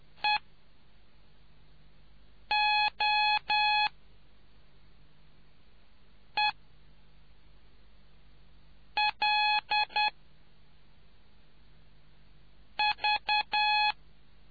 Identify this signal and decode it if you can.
Morse code, heard as a keyed tone, about 10 wpm, spaced as for 3 wpm, 840 Hz. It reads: EOELV